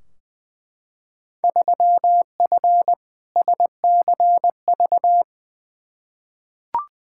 Morse code, heard as a keyed tone, about 20 wpm, 700 Hz. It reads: 3F SCV
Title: Field Day Exchanges - 20wpm